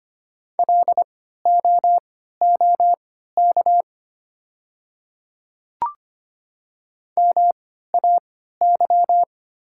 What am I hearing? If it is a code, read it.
Morse code, heard as a keyed tone, about 25 wpm, spaced as for 14 wpm, 700 Hz. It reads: LOOK MAY
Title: 3x Character Spacing - Top 500 Words - 25wpm